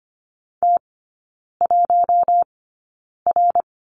Morse code, heard as a keyed tone, about 25 wpm, spaced as for 9 wpm, 700 Hz. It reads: T1R
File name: Lesson 14e - Number 2 - Three Characters 25wpm